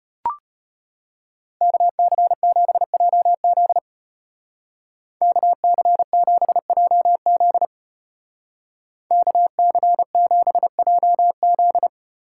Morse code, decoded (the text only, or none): KC7JZ KC7JZ KC7JZ